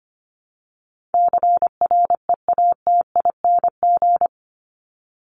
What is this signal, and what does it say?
Morse code, heard as a keyed tone, about 25 wpm, 700 Hz. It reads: CREATING
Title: ING Suffix - Top 100 Words with Base Words 25wpm